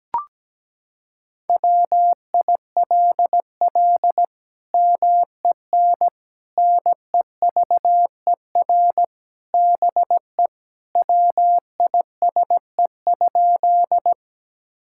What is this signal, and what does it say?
Morse code, heard as a keyed tone, about 17 wpm, 700 Hz. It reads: WILL MEN NEVER BE WISE?
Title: Sentences from Top 1600 Words 17wpm